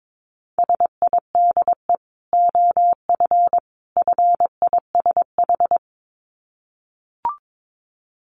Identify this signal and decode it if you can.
Morse code, heard as a keyed tone, about 22 wpm, 700 Hz. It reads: SIDE OF FISH
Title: Dit Challenge 22wpm